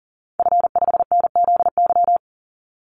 Morse code, heard as a keyed tone, about 40 wpm, 700 Hz. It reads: F5NZY
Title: Call Signs Repeated 2x - 40wpm